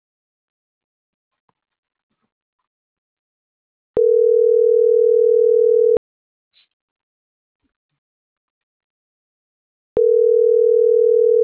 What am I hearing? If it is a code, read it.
Ringback tone (North America), 440 + 480 Hz.